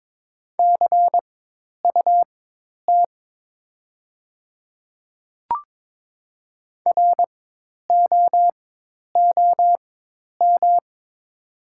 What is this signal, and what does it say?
Morse code, heard as a keyed tone, about 22 wpm, spaced as for 10 wpm, 690 Hz. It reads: CUT ROOM
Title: 4x Character Spacing - Top 300 Words - 22wpm